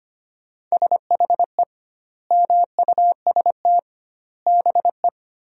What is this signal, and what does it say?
Morse code, heard as a keyed tone, about 25 wpm, 700 Hz. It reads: SHE MUST BE
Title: 2x Word-Spacing - Sentences from Top 400 Words - 25wpm